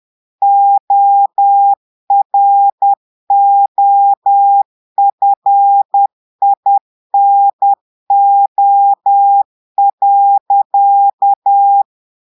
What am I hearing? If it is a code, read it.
Morse code, heard as a keyed tone, about 10 wpm, 800 Hz. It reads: OROFINO.
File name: Portugais - texte en clair 1 - vitesse - 10 mots minute